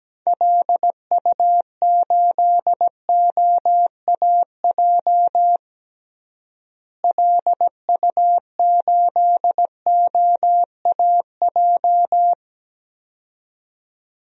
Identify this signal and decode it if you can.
Morse code, heard as a keyed tone, about 17 wpm, 700 Hz. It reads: LU8OAJ LU8OAJ
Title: DXCC-100 Call Signs - 6 Characters - Repeated 2x - 17wpm